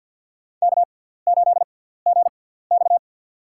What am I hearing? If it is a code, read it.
Morse code, heard as a keyed tone, about 47 wpm, spaced as for 17 wpm, 700 Hz. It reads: K8GX
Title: USA Call Signs - 4 Characters - 6x Character Spacing - 50wpm